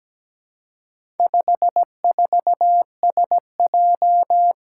Morse code, heard as a keyed tone, about 17 wpm, 700 Hz. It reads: 54SJ
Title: Four Challenging Letter-Number Combinations 17wpm